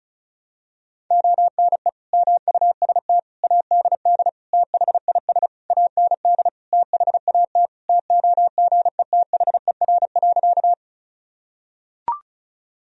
Morse code, heard as a keyed tone, about 34 wpm, 700 Hz. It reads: ONE MUST ADD THIS AND THAT TOGETHER.
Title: Sentences from Top 1000 Words - Review Part 1 - 35wpm